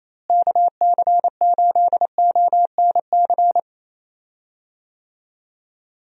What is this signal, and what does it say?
Morse code, heard as a keyed tone, about 28 wpm, 700 Hz. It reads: KC8ONC
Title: USA Call Signs - 6 Characters - Repeated 3x - 28wpm